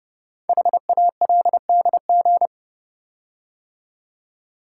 Morse code, heard as a keyed tone, about 29 wpm, 700 Hz. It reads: HALDG
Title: Code Groups - 5 Letters - 30wpm